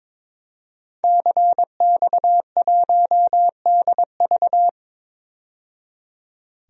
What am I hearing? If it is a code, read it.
Morse code, heard as a keyed tone, about 22 wpm, 700 Hz. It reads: CX1DV
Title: DXCC-100 Call Signs - 5 Characters - 22wpm